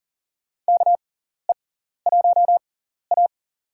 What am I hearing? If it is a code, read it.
Morse code, heard as a keyed tone, about 40 wpm, spaced as for 14 wpm, 700 Hz. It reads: KE1A
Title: USA Call Signs - 4 Characters - 6x Character Spacing - 40wpm